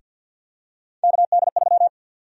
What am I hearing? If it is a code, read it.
Morse code, heard as a keyed tone, about 53 wpm, 700 Hz. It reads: KD3
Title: USA Call Sign Prefixes - 50wpm